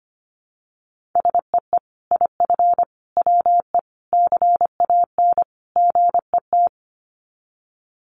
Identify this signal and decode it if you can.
Morse code, heard as a keyed tone, about 25 wpm, 700 Hz. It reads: SEE IF WE CAN GET
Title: Sets of 5 Words Encore2 25wpm